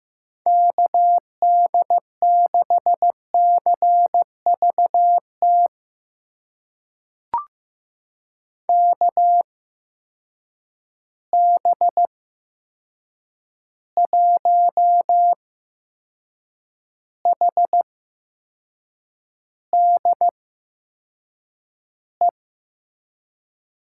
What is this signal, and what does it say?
Morse code, heard as a keyed tone, about 15 wpm, spaced as for 6 wpm, 700 Hz. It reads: KD6CVT KB1HDE
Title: USA Call Signs - 6 Characters - 8x Character Spacing - 15wpm